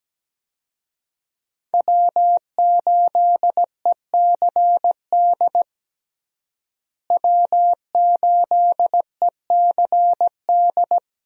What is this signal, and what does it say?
Morse code, heard as a keyed tone, about 17 wpm, 700 Hz. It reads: W8ECD W8ECD
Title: USA Call Signs - 5 Characters - Repeated 3x - 17wpm